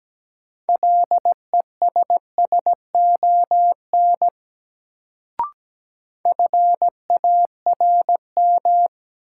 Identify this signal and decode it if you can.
Morse code, heard as a keyed tone, about 17 wpm, 700 Hz. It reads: LESSON FARM
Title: Sentences from Top 1200 Words - New Word Drill Alt - 17wpm